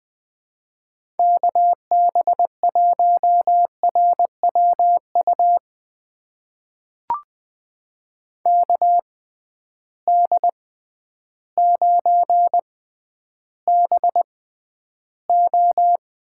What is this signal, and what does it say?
Morse code, heard as a keyed tone, about 20 wpm, spaced as for 11 wpm, 700 Hz. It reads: KB1RWU KD9BO
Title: USA Call Signs - 6 Characters - 6x Character Spacing - 20wpm